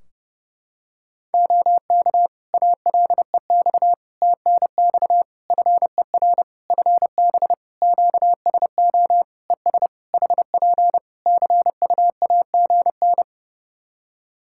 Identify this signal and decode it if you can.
Morse code, heard as a keyed tone, about 30 wpm, 700 Hz. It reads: OK ALEX TNX FER FB QSO ES HP CUAGN
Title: Practice QSO - Exchange 4 - Ending - 30wpm